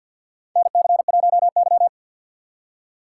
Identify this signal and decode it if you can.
Morse code, heard as a keyed tone, about 53 wpm, 700 Hz. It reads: NC1Y
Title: USA Call Signs - 4 Characters - 3x Character Spacing - 50wpm